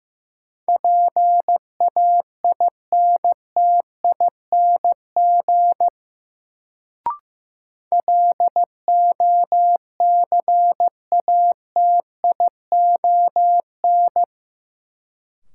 Morse code, heard as a keyed tone, about 15 wpm, 700 Hz. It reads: PAINTING LOCATION
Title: Sentences from Top 1000 Words - Review Part 6 - New Word Drill - 15wpm